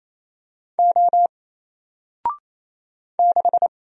Morse code, heard as a keyed tone, about 27 wpm, 700 Hz. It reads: O 6
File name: Single-Letter-Number-28wpm